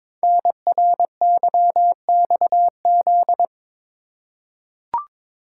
Morse code, heard as a keyed tone, about 22 wpm, 700 Hz. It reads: NRYXZ
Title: Code Groups - 5 Letters - 22wpm